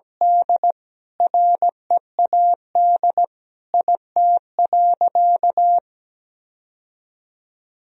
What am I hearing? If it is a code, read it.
Morse code, heard as a keyed tone, about 17 wpm, 700 Hz. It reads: D READ IT.
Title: Sentences from Top 1000 Words - Review Part 1 - 17wpm